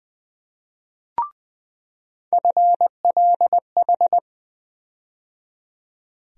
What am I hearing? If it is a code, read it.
Morse code, heard as a keyed tone, about 20 wpm, 700 Hz. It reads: FLH
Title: USA Call Sign Postfixes - 3 Letters 20wpm